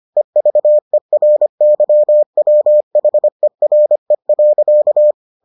Morse code, heard as a keyed tone, about 25 wpm, 600 Hz. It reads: EVERYWHERE.